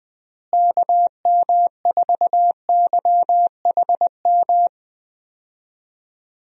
Morse code, heard as a keyed tone, about 20 wpm, 700 Hz. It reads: KM4YHM
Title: USA Call Signs - 6 Characters - Repeated 3x - 20wpm